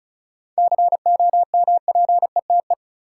Morse code, heard as a keyed tone, about 34 wpm, 700 Hz. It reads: COMPETE
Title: Seven Letter Words - 35wpm